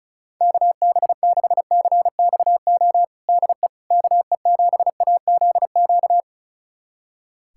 Morse code, heard as a keyed tone, about 36 wpm, 700 Hz. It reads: KB6CXO DE KE7AZQ
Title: USA Callsign DE Callsign - 6 Characters - 35wpm